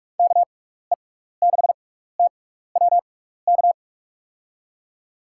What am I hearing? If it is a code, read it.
Morse code, heard as a keyed tone, about 44 wpm, spaced as for 15 wpm, 700 Hz. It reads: KE6TWK
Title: USA Call Signs - 6 Characters - 6x Character Spacing - 45wpm